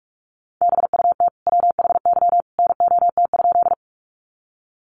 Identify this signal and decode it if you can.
Morse code, heard as a keyed tone, about 44 wpm, 700 Hz. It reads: BUT WHY NOT?
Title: Sentences from Top 600 Words 45wpm